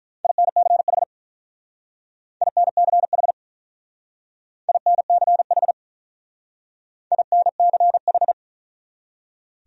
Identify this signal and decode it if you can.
Morse code, heard as a keyed tone, about 45 wpm, 700 Hz. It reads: INCH INCH INCH INCH